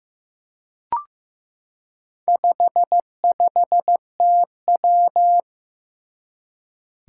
Morse code, heard as a keyed tone, about 15 wpm, 700 Hz. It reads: 55TW
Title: Four Challenging Letter-Number Combinations 15wpm